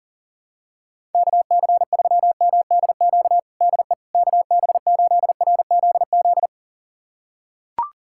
Morse code, heard as a keyed tone, about 37 wpm, 700 Hz. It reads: KC3MDQ DE KB8RZZ
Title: USA Callsign DE Callsign 40wpm